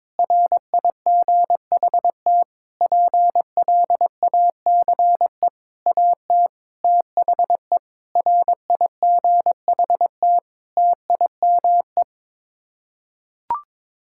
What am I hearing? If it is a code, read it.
Morse code, heard as a keyed tone, about 22 wpm, 700 Hz. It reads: RIGHT PLACE AT THE RIGHT TIME
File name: Sets of 9 Words - 22wpm